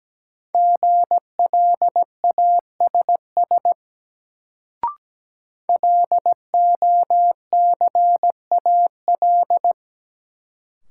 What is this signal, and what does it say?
Morse code, heard as a keyed tone, about 17 wpm, 700 Hz. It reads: GLASS LOCAL
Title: Sentences from Top 1000 Words - Review Part 3 - New Word Drill - 17wpm